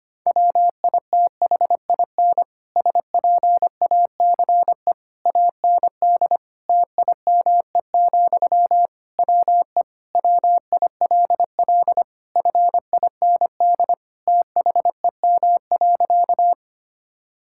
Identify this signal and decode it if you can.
Morse code, heard as a keyed tone, about 25 wpm, 700 Hz. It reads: WITHIN SPACE AND TIME, WE WILL FIND THEM.
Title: Sentences from Top 1000 Words - Review Part 5 - 25wpm